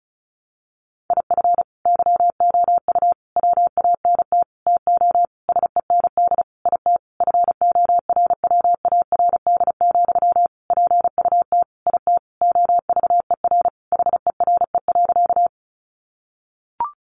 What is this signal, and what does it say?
Morse code, heard as a keyed tone, about 34 wpm, 700 Hz. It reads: IF YOU WANT TO SEND IT FORWARD, PUT IT OVER HERE.
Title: Sentences from Top 1500 Words 35wpm